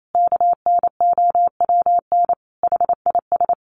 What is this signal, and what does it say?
Morse code, heard as a keyed tone, about 28 wpm, 700 Hz. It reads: KNOWN HIS
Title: Sentences from Top 1200 Words 28wpm